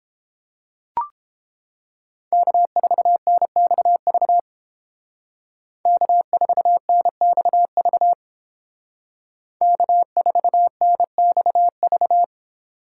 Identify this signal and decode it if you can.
Morse code, heard as a keyed tone, about 31 wpm, 700 Hz. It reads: K4NXV K4NXV K4NXV